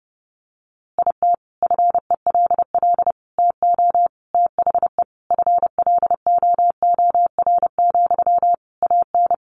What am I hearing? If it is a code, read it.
Morse code, heard as a keyed tone, about 30 wpm, 700 Hz. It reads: IT FELL TO THE FLOOR, AN
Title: A Sound Of Thunder 30wpm